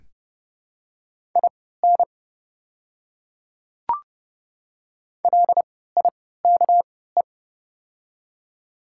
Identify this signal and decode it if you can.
Morse code, heard as a keyed tone, about 30 wpm, spaced as for 17 wpm, 700 Hz. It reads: IN LIKE
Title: 3x Character Spacing - Top 100 Words - 30wpm